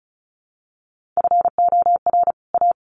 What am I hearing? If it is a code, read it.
Morse code, heard as a keyed tone, about 34 wpm, 700 Hz. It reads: FOR A
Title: Sets of 2 Words 35wpm